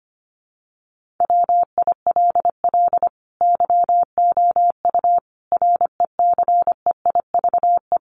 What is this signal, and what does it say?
Morse code, heard as a keyed tone, about 25 wpm, 700 Hz. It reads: WILL YOU RECEIVE
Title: Sentences from Top 1000 Words 25wpm